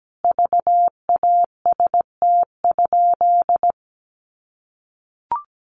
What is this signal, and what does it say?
Morse code, heard as a keyed tone, about 17 wpm, 700 Hz. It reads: VAST?
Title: Sentences from Top 1300 Words 17wpm